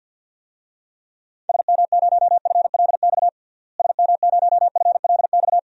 Morse code, heard as a keyed tone, about 49 wpm, 700 Hz. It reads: SM0FLX SM0FLX
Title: DXCC-100 Call Signs - 6 Characters - Repeated 3x - 50wpm